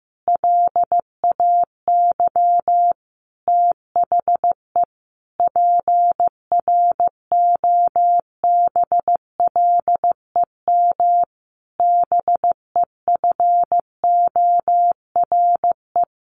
Morse code, heard as a keyed tone, about 15 wpm, 700 Hz. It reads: LAY THE PROBLEM BEFORE
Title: Sentences from Top 1500 Words 15wpm